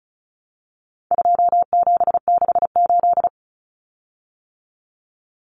Morse code, heard as a keyed tone, about 34 wpm, 700 Hz. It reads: 2768
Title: Common Cut Numbers - 4-Digit 35wpm